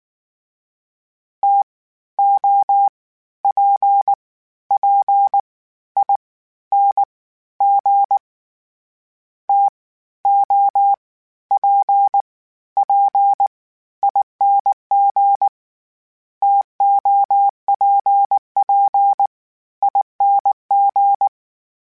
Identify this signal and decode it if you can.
Morse code, heard as a keyed tone, about 19 wpm, 800 Hz. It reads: T O P P I N G T O P P ING TOPP ING
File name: Variable Spacing / Variable Spacing-mp3s / TOs @18